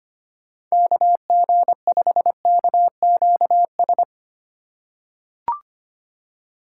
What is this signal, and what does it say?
Morse code, heard as a keyed tone, about 25 wpm, 700 Hz. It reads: KG5KQS
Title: USA Call Signs - 6 Characters - 2x Character Spacing - 25wpm